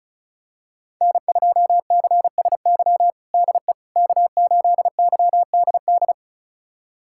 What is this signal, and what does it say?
Morse code, heard as a keyed tone, about 36 wpm, 700 Hz. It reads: N2CSY DE K8YDD
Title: USA Callsign DE Callsign - 5 Characters - 35wpm